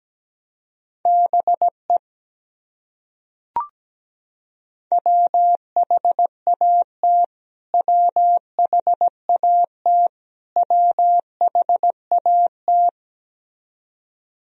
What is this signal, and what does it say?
Morse code, heard as a keyed tone, about 17 wpm, 700 Hz. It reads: BE WHAT WHAT WHAT
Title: Top 100 Words Repeated 17wpm